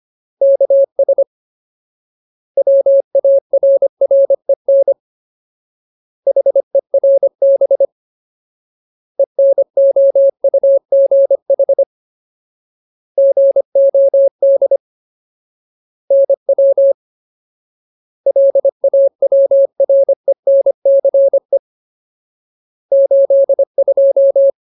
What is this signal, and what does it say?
Morse code, heard as a keyed tone, about 25 wpm, 550 Hz. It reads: KS WARREN HERB ENOUGH GOD NW LAWRENCE 82
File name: Ultimate Word List / 25WPM Ultimate